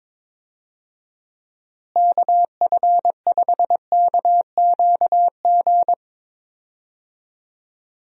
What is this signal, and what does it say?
Morse code, heard as a keyed tone, about 22 wpm, 700 Hz. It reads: KF5KQG
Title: USA Call Signs - 6 Characters - Repeated 2x - 22wpm